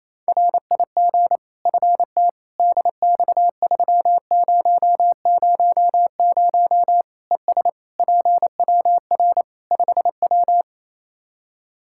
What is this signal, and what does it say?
Morse code, heard as a keyed tone, about 27 wpm, 700 Hz. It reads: RIG FT DX3000 ES PWR 5W
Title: Practice QSO - Exchange 2 - Info - 28wpm